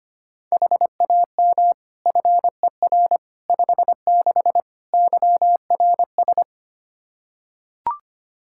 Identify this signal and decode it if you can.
Morse code, heard as a keyed tone, about 25 wpm, 700 Hz. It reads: HAM FER 56 YRS
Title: Practice QSO - Exchange 3 - More Info - 25wpm